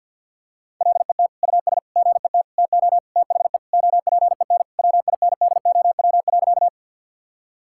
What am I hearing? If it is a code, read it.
Morse code, heard as a keyed tone, about 50 wpm, 700 Hz. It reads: LET US GET TO THE OPEN WINDOW.
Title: Sentences from Top 1600 Words 50wpm